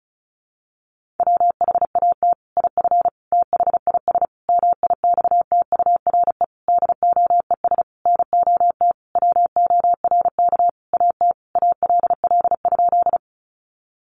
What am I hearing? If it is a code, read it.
Morse code, heard as a keyed tone, about 34 wpm, 700 Hz. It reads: WHAT IF THIS MIXTURE DOES NOT WORK AT ALL?
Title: Sentences from Top 1500 Words 35wpm